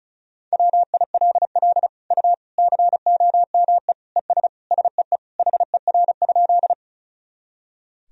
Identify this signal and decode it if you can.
Morse code, heard as a keyed tone, about 36 wpm, 700 Hz. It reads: WILL U COME ES SEE HER?
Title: Sentences from Top 100 Words and Abbreviations 35wpm